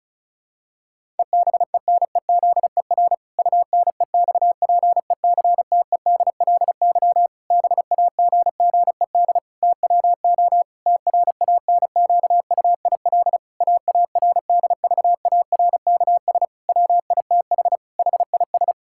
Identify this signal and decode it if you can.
Morse code, heard as a keyed tone, about 36 wpm, 700 Hz. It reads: EBENEZER UNEXPECTEDLY BAGGED TWO TRANQUIL AARDVARKS WITH HIS
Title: Pangram Challenge v2 - 35wpm